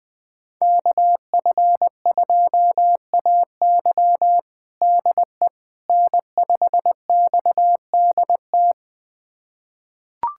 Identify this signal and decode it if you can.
Morse code, heard as a keyed tone, about 20 wpm, 700 Hz. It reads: KF2AY DE N5XDT
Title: USA Callsign DE Callsign - 5 Characters - 20wpm